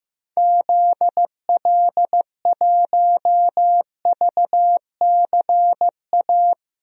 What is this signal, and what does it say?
Morse code, heard as a keyed tone, about 15 wpm, 700 Hz. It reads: ZL1VCA
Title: DXCC-100 Call Signs - 6 Characters - 15wpm